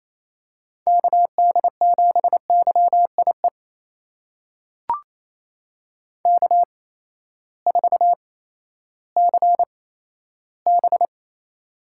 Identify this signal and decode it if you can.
Morse code, heard as a keyed tone, about 28 wpm, spaced as for 14 wpm, 700 Hz. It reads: KD7YIE K4CB
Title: USA Call Signs - 8x Character Spacing - 28wpm